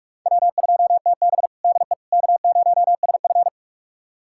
Morse code, heard as a keyed tone, about 43 wpm, 700 Hz. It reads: W2TB DE K0SF
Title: USA Callsign DE Callsign - 4 Characters - 45wpm